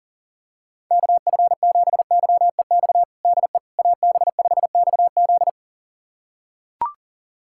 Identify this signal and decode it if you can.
Morse code, heard as a keyed tone, about 40 wpm, 700 Hz. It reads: KF7YEX DE AB5XZ
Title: USA Callsign DE Callsign 40wpm